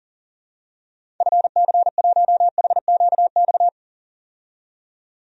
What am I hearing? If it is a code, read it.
Morse code, heard as a keyed tone, about 40 wpm, 700 Hz. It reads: FC1HQX
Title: DXCC-100 Call Signs - 6 Characters - 40wpm